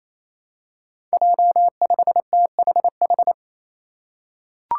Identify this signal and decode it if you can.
Morse code, heard as a keyed tone, about 28 wpm, 700 Hz. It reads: J5THH
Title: Five Challenging Letter-Number Combinations 28wpm